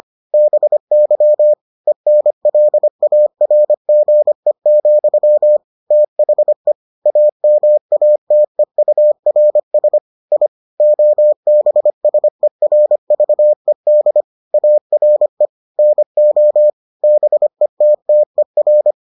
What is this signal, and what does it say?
Morse code, heard as a keyed tone, about 25 wpm, 600 Hz. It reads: BY ENLARGE, THE AMATEURS I OBSERVED ARE NO BETTER